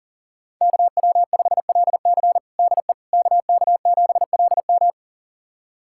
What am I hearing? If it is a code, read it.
Morse code, heard as a keyed tone, about 37 wpm, 700 Hz. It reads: KW5LC DE KK7LM
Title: USA Callsign DE Callsign - 5 Characters - 40wpm